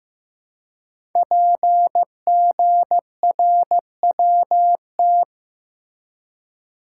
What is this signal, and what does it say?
Morse code, heard as a keyed tone, about 15 wpm, 700 Hz. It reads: PGRWT
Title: Code Groups - 5 Letters - 15wpm